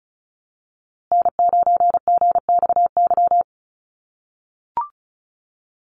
Morse code, heard as a keyed tone, about 34 wpm, 700 Hz. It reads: N9GXY
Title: USA Call Signs Repeated 3x 35wpm